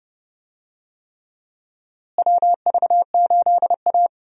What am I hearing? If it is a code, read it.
Morse code, heard as a keyed tone, about 29 wpm, 700 Hz. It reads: WV8A
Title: USA Call Signs - 4 Characters - Repeated 3x - 30wpm